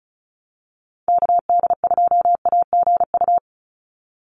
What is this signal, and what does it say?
Morse code, heard as a keyed tone, about 34 wpm, 700 Hz. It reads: KD2AGU
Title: USA Call Signs 35wpm